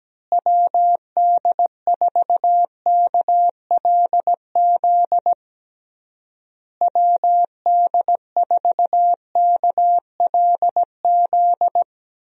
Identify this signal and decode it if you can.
Morse code, heard as a keyed tone, about 17 wpm, 700 Hz. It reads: WD4KLZ WD4KLZ
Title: USA Call Signs - 6 Characters - Repeated 3x - 17wpm